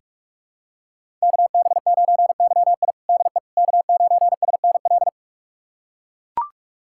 Morse code, heard as a keyed tone, about 42 wpm, 700 Hz. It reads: KB9YI DE K9SNL
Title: USA Callsign DE Callsign - 5 Characters - 45wpm